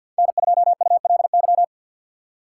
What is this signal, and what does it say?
Morse code, heard as a keyed tone, about 53 wpm, 700 Hz. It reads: N2ULY
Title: USA Call Signs - 5 Characters - 8x Character Spacing - 50wpm